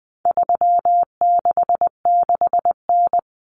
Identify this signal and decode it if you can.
Morse code, heard as a keyed tone, about 20 wpm, 700 Hz. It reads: 366N
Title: Common Cut Numbers - 4-Digit 20wpm